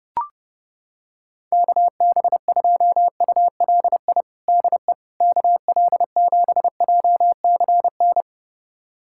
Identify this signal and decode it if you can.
Morse code, heard as a keyed tone, about 30 wpm, 700 Hz. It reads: KB2ULI DE KL7JCN
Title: USA Callsign DE Callsign 30wpm